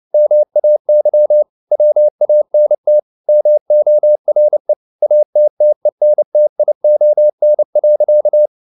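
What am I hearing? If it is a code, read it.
Morse code, heard as a keyed tone, about 29 wpm, 600 Hz. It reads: MAY WANT MORE ATTENTION.
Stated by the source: Morse code at 29 wpm, sending Ken Follett's novel Never in English, from CW med Gnister.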